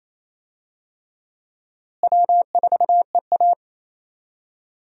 Morse code, heard as a keyed tone, about 28 wpm, 700 Hz. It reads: W4EA
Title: USA Call Signs - 4 Characters - Repeated 3x - 28wpm